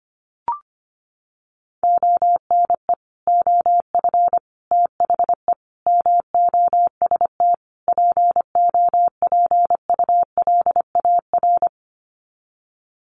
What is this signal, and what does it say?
Morse code, heard as a keyed tone, about 25 wpm, 700 Hz. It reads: ONE OF THE MOST POPULAR
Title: Sets of 5 Words 25wpm